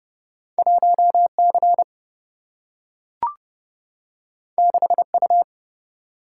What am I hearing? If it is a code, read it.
Morse code, heard as a keyed tone, about 30 wpm, 700 Hz. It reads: 1C 6U
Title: Single-Letter Single-Number 30wpm